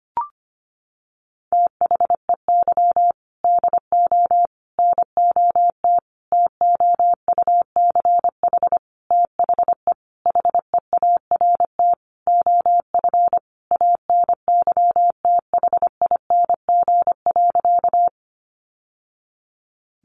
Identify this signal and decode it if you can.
Morse code, heard as a keyed tone, about 25 wpm, 700 Hz. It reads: THEY DO NOT TOUCH THE HEART OF ANYTHING.
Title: Sentences from Top 1200 Words 25wpm